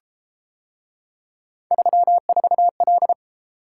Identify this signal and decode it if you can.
Morse code, heard as a keyed tone, about 33 wpm, 700 Hz. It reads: 34L